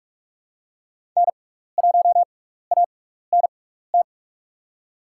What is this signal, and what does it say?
Morse code, heard as a keyed tone, about 47 wpm, spaced as for 16 wpm, 700 Hz. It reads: N1ANT
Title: USA Call Signs - 5 Characters - 6x Character Spacing - 45wpm